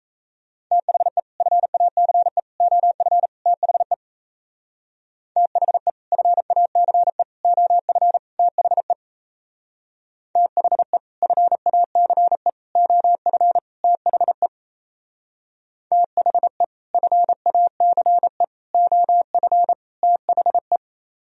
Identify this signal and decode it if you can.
Morse code, heard as a keyed tone, about 35 wpm, 700 Hz. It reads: THE FACE OF THE THE FACE OF THE THE FACE OF THE THE FACE OF THE
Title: Speed-Racing - Sets of 4 Words 28wpm